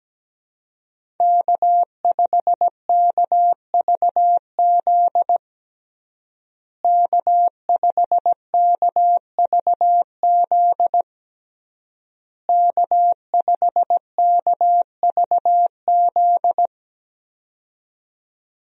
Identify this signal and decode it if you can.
Morse code, heard as a keyed tone, about 17 wpm, 700 Hz. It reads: K5KVZ K5KVZ K5KVZ
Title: USA Call Signs - 5 Characters - Repeated 3x - 17wpm